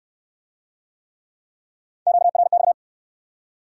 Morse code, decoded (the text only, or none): QRX